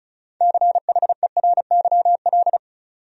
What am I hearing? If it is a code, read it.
Morse code, heard as a keyed tone, about 34 wpm, 700 Hz. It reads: CHERYL
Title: USA Common Names - 35wpm